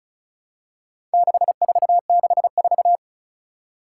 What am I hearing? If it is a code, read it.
Morse code, heard as a keyed tone, about 36 wpm, 700 Hz. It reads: 6464
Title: Four Challenging Letter-Number Combinations 35wpm